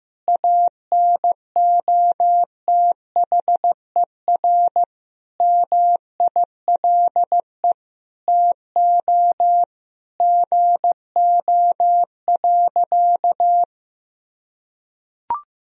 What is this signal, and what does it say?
Morse code, heard as a keyed tone, about 15 wpm, 700 Hz. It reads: ANOTHER MILE TO GO.
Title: Sentences from Top 1000 Words - Review Part 1 - 15wpm